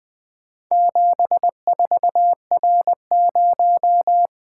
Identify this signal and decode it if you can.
Morse code, heard as a keyed tone, about 20 wpm, 700 Hz. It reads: 74R0
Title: Four Letter-Number Combinations - 20wpm